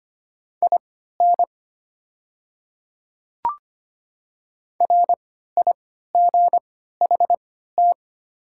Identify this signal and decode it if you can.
Morse code, heard as a keyed tone, about 25 wpm, spaced as for 14 wpm, 700 Hz. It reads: IN RIGHT
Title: 3x Character Spacing - Top 200 Words - 25wpm